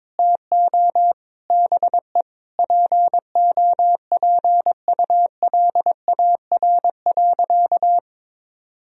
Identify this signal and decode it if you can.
Morse code, heard as a keyed tone, about 22 wpm, 700 Hz. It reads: TO BE POPULAR.
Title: Sentences from Top 1000 Words - Review Part 3 - 22wpm